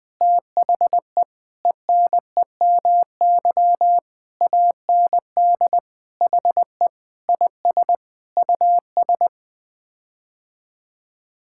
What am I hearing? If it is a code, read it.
Morse code, heard as a keyed tone, about 21 wpm, 700 Hz. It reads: THE ENEMY AND HE IS US
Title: Sets of 9 Words - 20wpm